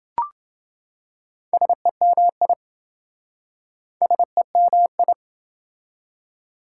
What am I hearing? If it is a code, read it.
Morse code, heard as a keyed tone, about 27 wpm, 700 Hz. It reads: SEMI SEMI